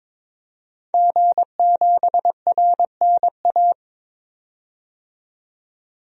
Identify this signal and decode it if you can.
Morse code, heard as a keyed tone, about 22 wpm, 700 Hz. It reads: G7RNA
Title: DXCC-100 Call Signs - 5 Characters - 22wpm